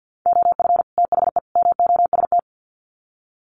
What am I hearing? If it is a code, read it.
Morse code, heard as a keyed tone, about 50 wpm, 700 Hz. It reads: OF THE MOST
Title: Sets of 3 Words 50wpm